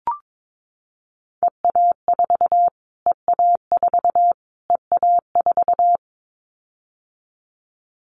Morse code, heard as a keyed tone, about 22 wpm, 700 Hz. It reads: EA4 EA4 EA4
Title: Callsign Prefixes for Top 25 DXCC - Repeated 3x - 22wpm